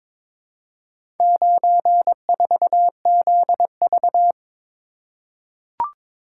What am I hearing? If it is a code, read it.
Morse code, heard as a keyed tone, about 22 wpm, 700 Hz. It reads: 94ZV
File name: practice / 22wpm / Four Letter-Number Combinations - 22wpm